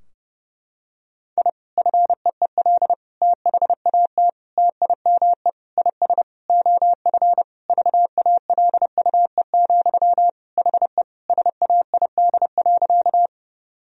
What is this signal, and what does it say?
Morse code, heard as a keyed tone, about 30 wpm, 700 Hz. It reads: I FEEL THAT TIME IS OF VALUE, HE SAID.
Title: Sentences from Top 1000 Words - Review Part 3 - 30wpm